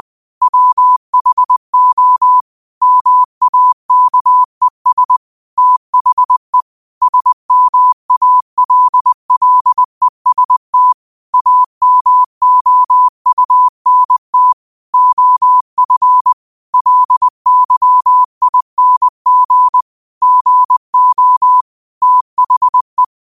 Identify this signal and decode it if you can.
Morse code, heard as a keyed tone, about 20 wpm, 1000 Hz. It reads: WHO MAKES THE SMALLEST AMOUNT OF LYING GO THE
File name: qotdfiles / 20wpm